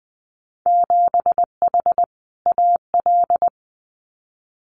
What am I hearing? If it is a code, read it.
Morse code, heard as a keyed tone, about 20 wpm, 700 Hz. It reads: 7H AL
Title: Winter Field Day Exchanges Encore 20wpm